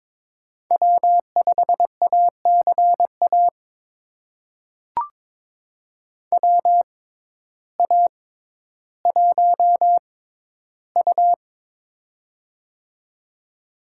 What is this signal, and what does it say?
Morse code, heard as a keyed tone, about 22 wpm, spaced as for 12 wpm, 700 Hz. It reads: W5ACA WA1U
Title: USA Call Signs - 6x Character Spacing - 22wpm